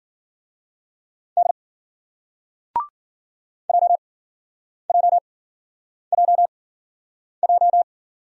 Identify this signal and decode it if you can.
Morse code, heard as a keyed tone, about 56 wpm, 700 Hz. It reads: D J J J J